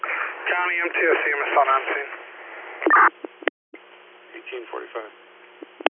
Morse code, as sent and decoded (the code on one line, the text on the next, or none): none